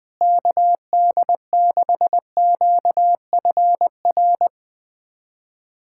Morse code, heard as a keyed tone, about 21 wpm, 700 Hz. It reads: KD6QFR
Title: USA Callsign DE Callsign 20wpm